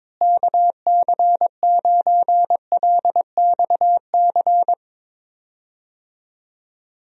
Morse code, heard as a keyed tone, about 22 wpm, 700 Hz. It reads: KC9LXC
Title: USA Call Signs - 6 Characters - Repeated 2x - 22wpm